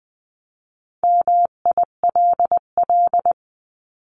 Morse code, heard as a keyed tone, about 20 wpm, 700 Hz. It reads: MILL
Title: 259 Lesson 42d - Words 18-20wpm